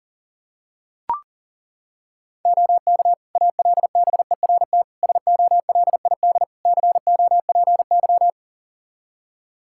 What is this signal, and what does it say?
Morse code, heard as a keyed tone, about 40 wpm, 700 Hz. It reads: OK ALBERT SOLID COPY
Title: Practice QSO - Exchange 3 - More Info - 40wpm